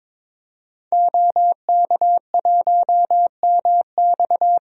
Morse code, heard as a keyed tone, about 22 wpm, 700 Hz. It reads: OK1MX
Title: DXCC-100 Call Signs - 5 Characters - 22wpm